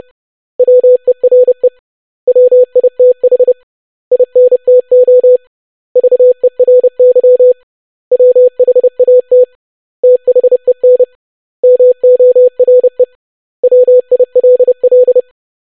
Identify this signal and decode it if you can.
Morse code, heard as a keyed tone, about 30 wpm, 500 Hz. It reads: WERE WITH INTO VERY WHAT THEN MORE WILL
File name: basic100-25wpm